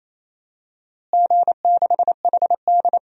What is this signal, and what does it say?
Morse code, heard as a keyed tone, about 28 wpm, 700 Hz. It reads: G6HD
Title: DXCC-100 Call Signs - 4 Characters - Repeated 2x - 28wpm